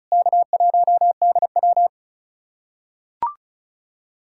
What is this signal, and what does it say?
Morse code, heard as a keyed tone, about 36 wpm, 700 Hz. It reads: K1DW
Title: USA Callsign DE Callsign - 4 Characters - 35wpm